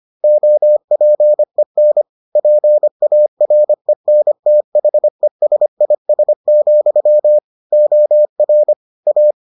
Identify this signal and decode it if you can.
Morse code, heard as a keyed tone, about 25 wpm, 600 Hz. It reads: OPEN PARENTHESIS, OR A